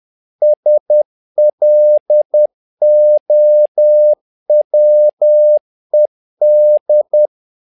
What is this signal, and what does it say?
Morse code, heard as a keyed tone, about 10 wpm, 600 Hz. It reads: SLOWED
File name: learning center / tower trouble / tower trouble 10wpm